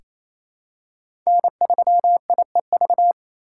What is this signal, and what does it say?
Morse code, heard as a keyed tone, about 28 wpm, 700 Hz. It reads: N3IEV
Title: USA Call Signs - 5 Characters - 4x Character Spacing - 28wpm